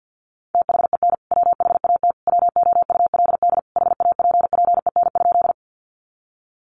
Morse code, heard as a keyed tone, about 50 wpm, 700 Hz. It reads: THEN WHAT WOULD HAPPEN?
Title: Sentences from Top 400 Words 50wpm